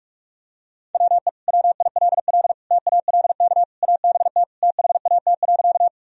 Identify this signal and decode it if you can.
Morse code, heard as a keyed tone, about 46 wpm, 700 Hz. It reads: WE WILL TALK ABT THAT.
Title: Sentences from Top 400 Words and Abbreviations 45wpm